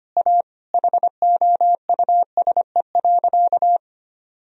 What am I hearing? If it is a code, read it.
Morse code, heard as a keyed tone, about 25 wpm, 700 Hz. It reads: A HOUSE.
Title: Sentences from Top 1000 Words - Review Part 1 - 25wpm